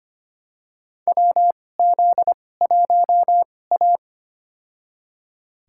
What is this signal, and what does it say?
Morse code, heard as a keyed tone, about 25 wpm, spaced as for 18 wpm, 700 Hz. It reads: WZ1A
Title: USA Call Signs - 4 Characters - 2x Character Spacing - 25wpm